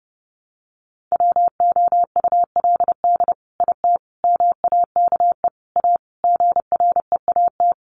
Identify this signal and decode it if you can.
Morse code, heard as a keyed tone, about 30 wpm, 700 Hz. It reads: WOULD IT MAKE A GREAT
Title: Sentences from Top 800 Words 30wpm